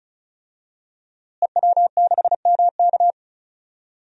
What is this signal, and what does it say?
Morse code, heard as a keyed tone, about 34 wpm, 700 Hz. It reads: EW6MK
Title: DXCC-100 Call Signs - 5 Characters - Repeated 3x - 35wpm